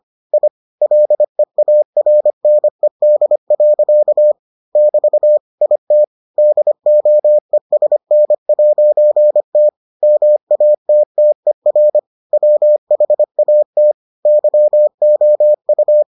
Morse code, heard as a keyed tone, about 25 wpm, 600 Hz. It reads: I LEARNED. <BT> IT DOESN'T MATTER WHAT YOU